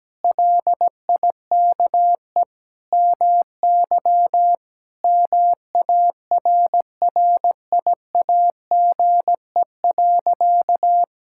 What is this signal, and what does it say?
Morse code, heard as a keyed tone, about 17 wpm, 700 Hz. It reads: LIKE MY MARRIAGE.
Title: Sentences from Top 1000 Words - Review Part 5 - 17wpm